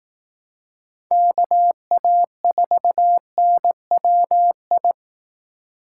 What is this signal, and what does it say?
Morse code, heard as a keyed tone, about 18 wpm, 700 Hz. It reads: KA4NWI
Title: Speed Racing - USA 2x3 Call Signs 15wpm